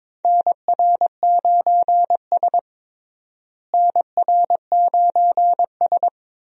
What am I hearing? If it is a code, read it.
Morse code, heard as a keyed tone, about 21 wpm, 700 Hz. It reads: NR9S NR9S
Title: USA Call Signs - 4 Characters - Repeated 2x - 22wpm